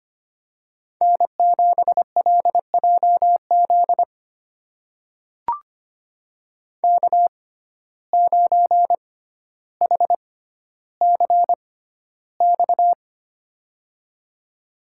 Morse code, heard as a keyed tone, about 25 wpm, spaced as for 13 wpm, 700 Hz. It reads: N7LJZ K9HCX
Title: USA Call Signs - 5 Characters - 6x Character Spacing - 25wpm